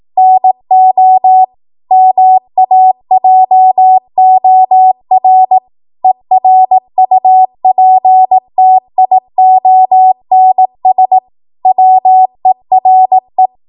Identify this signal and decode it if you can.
Morse code, heard as a keyed tone, about 18 wpm, 750 Hz. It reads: NO MAJOR ERUPTIONS WERE
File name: Morse / 18WPM